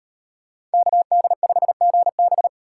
Morse code, heard as a keyed tone, about 38 wpm, 700 Hz. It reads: KD5GB